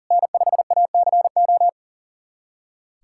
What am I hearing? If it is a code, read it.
Morse code, heard as a keyed tone, about 40 wpm, 700 Hz. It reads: N5ACO
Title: USA Callsign DE Callsign - 5 Characters - 40wpm